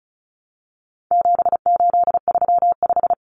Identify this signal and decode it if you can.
Morse code, heard as a keyed tone, about 34 wpm, 700 Hz. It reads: 7835
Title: Numbers - 4-Digit 35wpm